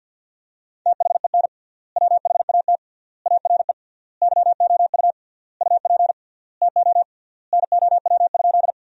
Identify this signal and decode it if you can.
Morse code, heard as a keyed tone, about 52 wpm, 700 Hz. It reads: THEN WHAT ARE YOU UP TO NOW?
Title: 3x Word-Spacing - Sentences from Top 100 Words - 50wpm